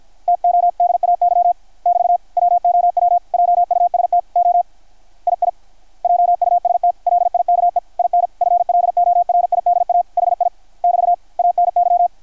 Morse code, heard as a keyed tone, about 53 wpm, 660 Hz. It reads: TODAY <BT> WOW JUST O II JUST LIKE IN FLORIDA HI <BT> ANY
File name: archived / CRSnet